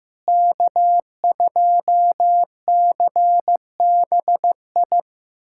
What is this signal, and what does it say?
Morse code, heard as a keyed tone, about 15 wpm, 700 Hz. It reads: K2CBI
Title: USA Callsign DE Callsign - 5 Characters - 15wpm